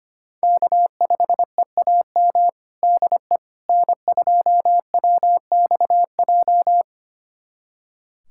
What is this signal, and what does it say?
Morse code, heard as a keyed tone, about 25 wpm, 700 Hz. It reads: K5EAM DE N2WXJ
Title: USA Callsign DE Callsign - 5 Characters - 25wpm